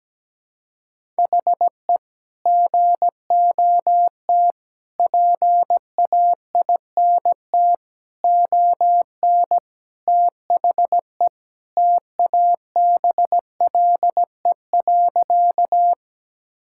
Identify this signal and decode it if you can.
Morse code, heard as a keyed tone, about 17 wpm, 700 Hz. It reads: HE GOT PAINT ON THE TABLE.
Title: Sentences from Top 1000 Words - Review Part 6 - 17wpm